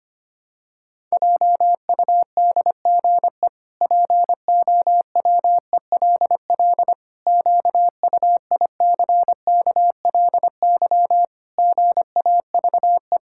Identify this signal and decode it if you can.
Morse code, heard as a keyed tone, about 25 wpm, 700 Hz. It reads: JUDGE POWELL QUICKLY GAVE
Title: Pangram Challenge v2 - 25wpm